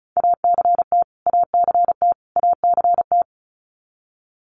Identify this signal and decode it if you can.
Morse code, heard as a keyed tone, about 34 wpm, 700 Hz. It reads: ACT ACT ACT
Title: Top 300 Words Repeated 35wpm